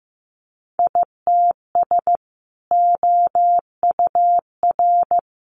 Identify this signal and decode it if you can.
Morse code, heard as a keyed tone, about 15 wpm, 700 Hz. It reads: ITS OUR
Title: Sentences from Top 600 Words 15wpm